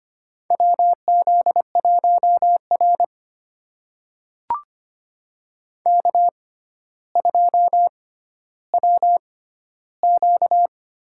Morse code, heard as a keyed tone, about 25 wpm, spaced as for 13 wpm, 700 Hz. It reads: WZ1R K2WQ
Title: USA Call Signs - 4 Characters - 6x Character Spacing - 25wpm